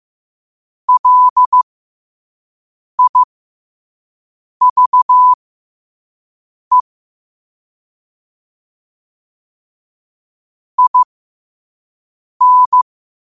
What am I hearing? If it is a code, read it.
Morse code, heard as a keyed tone, about 15 wpm, spaced as for 5 wpm, 1000 Hz. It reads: LIVE IN
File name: qotdfiles / 5wpm